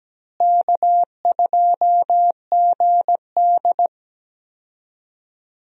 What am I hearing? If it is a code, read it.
Morse code, heard as a keyed tone, about 17 wpm, 700 Hz. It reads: K2GD
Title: USA Call Signs - 4 Characters - Repeated 2x - 17wpm